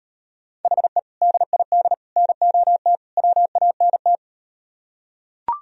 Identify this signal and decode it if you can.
Morse code, heard as a keyed tone, about 38 wpm, 700 Hz. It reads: HE DID NOT WANT